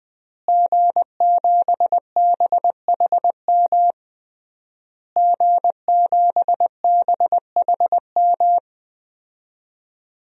Morse code, heard as a keyed tone, about 20 wpm, 700 Hz. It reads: G7BHM G7BHM
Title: DXCC-100 Call Signs - 5 Characters - Repeated 3x - 20wpm